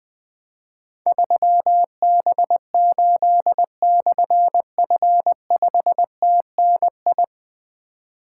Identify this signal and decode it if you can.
Morse code, heard as a keyed tone, about 20 wpm, 700 Hz. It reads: 3B8/F5TNI
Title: Difficult Call Signs 20wpm